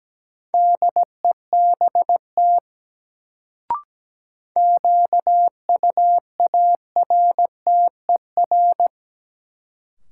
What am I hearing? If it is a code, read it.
Morse code, heard as a keyed tone, about 17 wpm, 700 Hz. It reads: DEBT QUARTER
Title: Sentences from Top 1300 Words - New Word Drill Alt - 17wpm